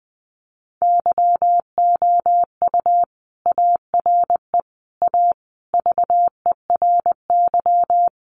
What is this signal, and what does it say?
Morse code, heard as a keyed tone, about 20 wpm, 700 Hz. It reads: YOU ARE A VERY
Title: Sentences from Top 1100 Words 20wpm